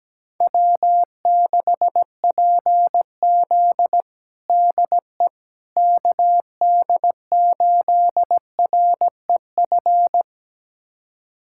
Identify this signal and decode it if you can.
Morse code, heard as a keyed tone, about 17 wpm, 700 Hz. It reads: W6PZ DE KD8REF
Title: USA Callsign DE Callsign 17wpm